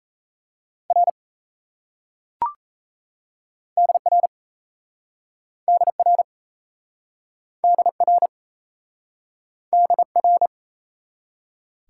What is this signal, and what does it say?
Morse code, heard as a keyed tone, about 37 wpm, 700 Hz. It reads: R DR DR DR DR